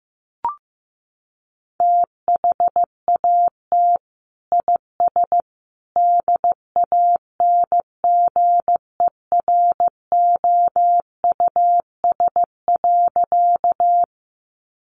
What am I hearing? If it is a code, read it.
Morse code, heard as a keyed tone, about 15 wpm, 700 Hz. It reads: THAT IS DANGEROUS.
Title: Sentences from Top 1200 Words 15wpm